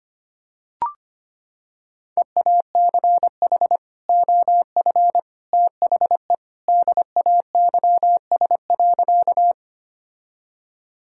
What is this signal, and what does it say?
Morse code, heard as a keyed tone, about 25 wpm, 700 Hz. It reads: EACH OF THE DAYS.
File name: practice / 25wpm / Sentences with 3-Word Phrases - 25wpm